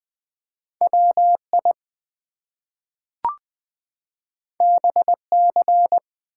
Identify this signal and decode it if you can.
Morse code, heard as a keyed tone, about 20 wpm, 700 Hz. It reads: WI BC
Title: US States and Canadian Provinces 20wpm